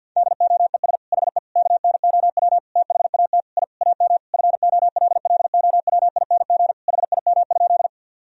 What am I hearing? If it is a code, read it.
Morse code, heard as a keyed tone, about 50 wpm, 700 Hz. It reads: DOES HE KNOW THAT I AM FOLLOWING HIM?
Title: Sentences from Top 1000 Words - Review Part 4 - 50wpm